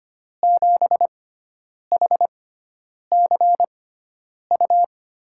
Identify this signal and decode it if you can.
Morse code, heard as a keyed tone, about 25 wpm, spaced as for 9 wpm, 700 Hz. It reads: 7HCU
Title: USA Call Signs - 6 Characters - 6x Character Spacing - 25wpm